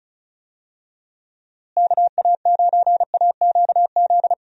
Morse code, heard as a keyed tone, about 36 wpm, 700 Hz. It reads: KA9AQZ
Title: USA Call Signs - 6 Characters - Repeated 3x - 35wpm